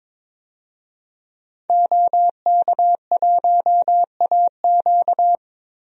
Morse code, heard as a keyed tone, about 22 wpm, 700 Hz. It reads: OK1AQ
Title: DXCC-100 Call Signs - 5 Characters - Repeated 2x - 22wpm